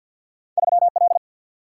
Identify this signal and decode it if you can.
Morse code, heard as a keyed tone, about 50 wpm, 700 Hz. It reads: 3 L
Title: Single-Letter Single-Number 50wpm